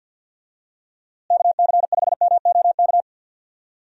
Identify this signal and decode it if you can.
Morse code, heard as a keyed tone, about 52 wpm, 700 Hz. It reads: KC5MOK
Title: USA Call Signs - 6 Characters - Repeated 3x - 50wpm